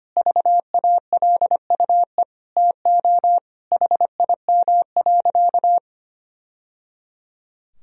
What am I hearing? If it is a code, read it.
Morse code, heard as a keyed tone, about 25 wpm, 700 Hz. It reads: VALUE TO HIM.
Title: Sentences from Top 1000 Words - Review Part 3 - 25wpm